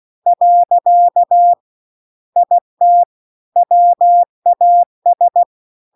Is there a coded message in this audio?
Yes: Morse code from 0.3 to 5.4 s, heard as a keyed tone, about 16 wpm, 700 Hz. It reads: . IT WAS